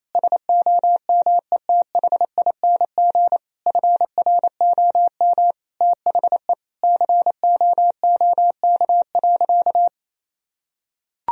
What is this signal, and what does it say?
Morse code, heard as a keyed tone, about 28 wpm, 700 Hz. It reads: SOMETHING FROM THE COOK.
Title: Sentences from Top 1000 Words - Review Part 4 - 28wpm